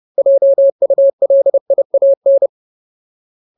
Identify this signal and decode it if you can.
Morse code, heard as a keyed tone, about 30 wpm, 550 Hz. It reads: JULIAN
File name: Ultimate Word List / 30WPM Ultimate